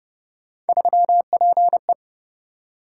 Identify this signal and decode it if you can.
Morse code, heard as a keyed tone, about 29 wpm, 700 Hz. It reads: 3PE